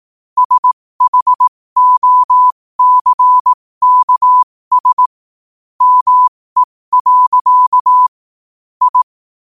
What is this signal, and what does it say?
Morse code, heard as a keyed tone, about 18 wpm, spaced as for 15 wpm, 1000 Hz. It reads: SHOCKS ME. I
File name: qotdfiles / 15wpm